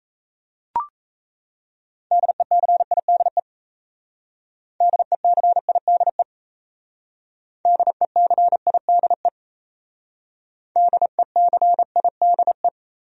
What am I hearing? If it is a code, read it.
Morse code, heard as a keyed tone, about 38 wpm, 700 Hz. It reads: DECIDE DECIDE DECIDE DECIDE